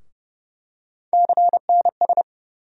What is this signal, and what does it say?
Morse code, heard as a keyed tone, about 30 wpm, 700 Hz. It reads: CNS
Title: USA Call Sign Postfixes - 3 Letters 30wpm